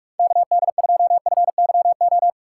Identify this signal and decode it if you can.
Morse code, heard as a keyed tone, about 44 wpm, 700 Hz. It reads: KD2FYO
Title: USA Call Signs - 6 Characters - 3x Character Spacing - 45wpm